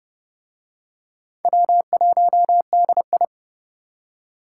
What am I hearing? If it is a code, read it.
Morse code, heard as a keyed tone, about 25 wpm, 700 Hz. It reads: W1DI
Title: Speed Racing - USA 1x2 Call Signs 20wpm